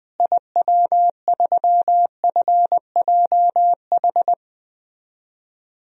Morse code, heard as a keyed tone, about 20 wpm, 700 Hz. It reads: IW3FJH
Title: DXCC-100 Call Signs - 6 Characters - Repeated 3x - 20wpm